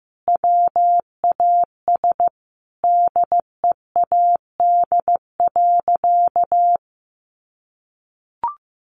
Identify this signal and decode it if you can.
Morse code, heard as a keyed tone, about 15 wpm, 700 Hz. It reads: WAS DEAD.
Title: Sentences from Top 1500 Words 15wpm